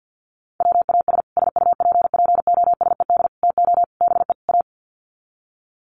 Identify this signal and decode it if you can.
Morse code, heard as a keyed tone, about 50 wpm, 700 Hz. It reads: WAS SUPPOSED TO BE A
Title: Sets of 5 Words Encore 50wpm